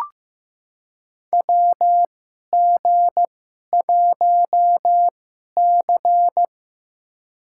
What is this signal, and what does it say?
Morse code, heard as a keyed tone, about 16 wpm, spaced as for 11 wpm, 700 Hz. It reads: WG1C
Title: USA Call Signs - 2x Character Spacing - 15wpm